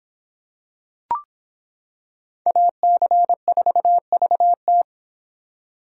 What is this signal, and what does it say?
Morse code, heard as a keyed tone, about 25 wpm, 700 Hz. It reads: AC4VT